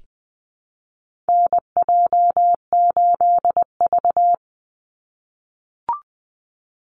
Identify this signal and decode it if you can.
Morse code, heard as a keyed tone, about 19 wpm, 700 Hz. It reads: NJ8V
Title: Call Signs - 20wpm